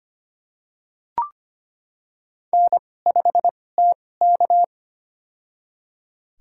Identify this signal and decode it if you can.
Morse code, heard as a keyed tone, about 25 wpm, spaced as for 18 wpm, 700 Hz. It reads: N5TK
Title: USA Call Signs - 4 Characters - 2x Character Spacing - 25wpm